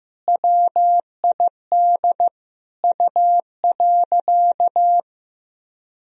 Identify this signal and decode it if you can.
Morse code, heard as a keyed tone, about 15 wpm, 700 Hz. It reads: WID U.
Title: Sentences from Top 200 Words and Abbreviations 15wpm